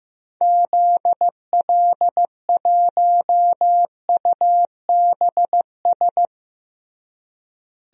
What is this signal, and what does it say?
Morse code, heard as a keyed tone, about 15 wpm, 700 Hz. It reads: ZL1UBS
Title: DXCC-100 Call Signs - 6 Characters - Repeated 3x - 15wpm